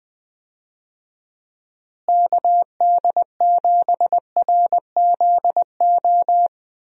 Morse code, heard as a keyed tone, about 21 wpm, 700 Hz. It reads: KD7RZO
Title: USA Call Signs - 6 Characters - Repeated 3x - 20wpm